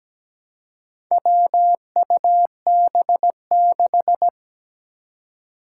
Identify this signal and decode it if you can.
Morse code, heard as a keyed tone, about 17 wpm, 700 Hz. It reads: WUB6
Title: Four Challenging Letter-Number Combinations 17wpm